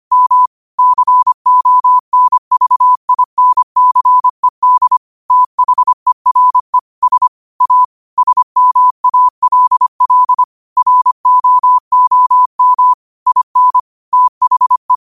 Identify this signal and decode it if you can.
Morse code, heard as a keyed tone, about 25 wpm, 1000 Hz. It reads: M CONVINCED THERE S A SMALL ROOM IN THE
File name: 25wpm